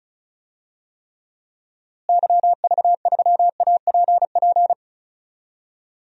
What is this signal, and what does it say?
Morse code, heard as a keyed tone, about 34 wpm, 700 Hz. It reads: YV3APP
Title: DXCC-100 Call Signs - 6 Characters - Repeated 2x - 35wpm